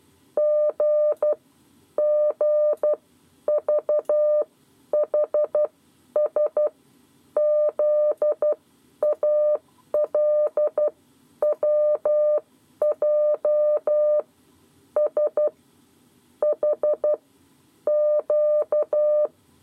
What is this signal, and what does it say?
Morse code, heard as a keyed tone, about 12 wpm, spaced as for 9 wpm, 600 Hz. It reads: GGVHS ZALWJ SHQ
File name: 7 Lezione cw 2